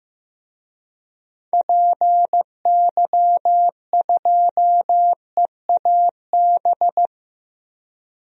Morse code, heard as a keyed tone, about 15 wpm, 700 Hz. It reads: PY2EAB
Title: DXCC-100 Call Signs - 6 Characters - 15wpm